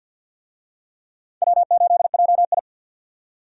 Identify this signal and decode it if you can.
Morse code, heard as a keyed tone, about 49 wpm, 700 Hz. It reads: W8JI
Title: USA Call Signs - 4 Characters - 50wpm